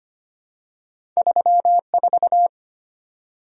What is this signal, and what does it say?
Morse code, heard as a keyed tone, about 25 wpm, 700 Hz. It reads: 34
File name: Two Letter-Number Combinations - 25wpm